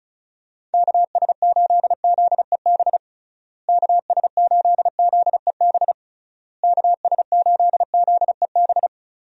Morse code, heard as a keyed tone, about 36 wpm, 700 Hz. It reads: KS8ZEB KS8ZEB KS8ZEB
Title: USA Call Signs - 6 Characters - Repeated 3x - 35wpm